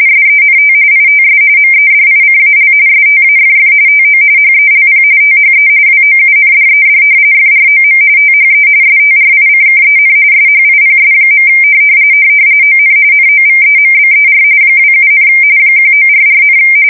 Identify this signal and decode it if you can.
RTTY, about 45.45 baud. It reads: SENT ITS MEDICAL STAFF TO HELP THE WOUNDED FROM THE STRICKEN SHIPS. ON NOVEMBER 27, 1944,
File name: fsk170 / battleshipnj